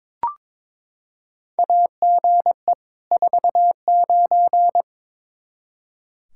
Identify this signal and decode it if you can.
Morse code, heard as a keyed tone, about 22 wpm, 700 Hz. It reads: AGE 49
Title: Practice QSO - Exchange 3 - More Info - 22wpm